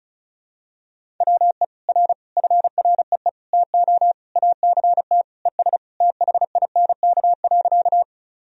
Morse code, heard as a keyed tone, about 36 wpm, 700 Hz. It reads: WE R FREE TO ACT ES THINK.
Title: Sentences from Top 500 Words and Abbreviations 35wpm